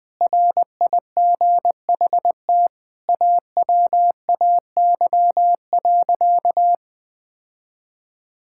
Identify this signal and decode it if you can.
Morse code, heard as a keyed tone, about 20 wpm, 700 Hz. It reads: RIGHT AWAY.
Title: Sentences from Top 1600 Words 20wpm